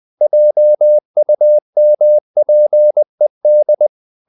Morse code, heard as a keyed tone, about 20 wpm, 600 Hz. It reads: JUMPED
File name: the old man and the sea part 2 20wpm